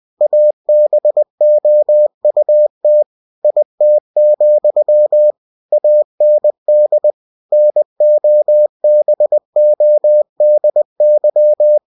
Morse code, heard as a keyed tone, about 20 wpm, 600 Hz. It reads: ABOUT IT, AND NOBODY
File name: cw-books / winnie-the-pooh / winnie-20wpm-page-0103